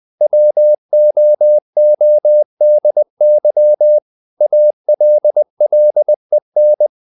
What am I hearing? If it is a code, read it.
Morse code, heard as a keyed tone, about 20 wpm, 600 Hz. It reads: WOODY ALLEN
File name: CW / Practice / EnglishQuotations0320